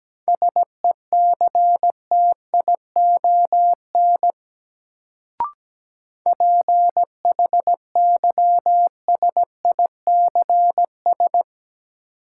Morse code, heard as a keyed tone, about 17 wpm, 700 Hz. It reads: SECTION PHYSICS
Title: Sentences from Top 1000 Words - Review Part 5 - New Word Drill - 17wpm